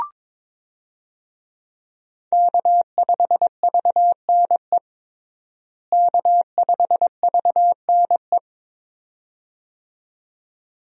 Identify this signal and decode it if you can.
Morse code, heard as a keyed tone, about 22 wpm, 700 Hz. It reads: K5VNE K5VNE
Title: USA Call Signs - 5 Characters - Repeated 2x - 22wpm